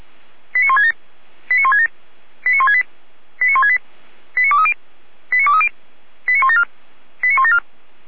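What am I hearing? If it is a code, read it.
Five-tone selective call (ZVEI), heard as quick runs of five tones, tones of about 70 ms. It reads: 89167 89167 89168 89168 89230 89230 89174 89174
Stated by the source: ZVEI five-tone selective call received on an SDR.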